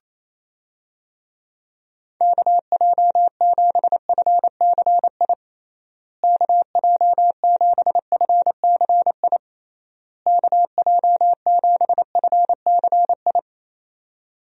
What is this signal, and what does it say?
Morse code, heard as a keyed tone, about 28 wpm, 700 Hz. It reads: KJ7FCI KJ7FCI KJ7FCI
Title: USA Call Signs - 6 Characters - Repeated 3x - 28wpm